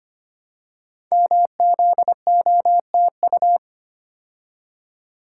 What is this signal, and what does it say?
Morse code, heard as a keyed tone, about 25 wpm, 690 Hz. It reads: MZOTU
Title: Code Groups - 5 Letters - 25wpm